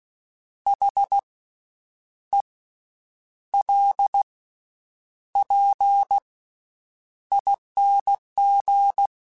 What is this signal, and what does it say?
Morse code, heard as a keyed tone, about 16 wpm, 800 Hz. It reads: H E L P ING
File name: Special INGs-mp3s / Special INGs-mp3s / Slow ing @15